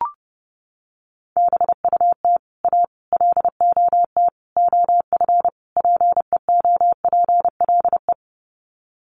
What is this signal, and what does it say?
Morse code, heard as a keyed tone, about 30 wpm, 700 Hz. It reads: BUT A LOT OF PEOPLE
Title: Sets of 5 Words Encore2 30wpm